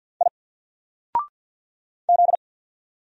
Morse code, heard as a keyed tone, about 48 wpm, 700 Hz. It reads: I Z
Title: Single Letter-Number 50wpm